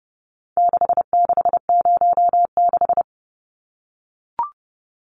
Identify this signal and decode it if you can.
Morse code, heard as a keyed tone, about 30 wpm, 700 Hz. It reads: 6606
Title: Numbers - 4-Digit 30wpm